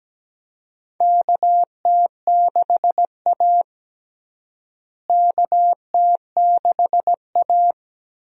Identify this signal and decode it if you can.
Morse code, heard as a keyed tone, about 17 wpm, 700 Hz. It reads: KT6A KT6A
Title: USA Call Signs - 4 Characters - Repeated 2x - 17wpm